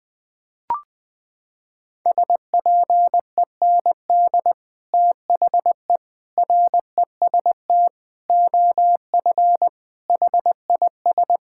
Morse code, heard as a keyed tone, about 20 wpm, 700 Hz. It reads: SPEND THE REST OF HIS
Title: Sets of 8 Words - 20wpm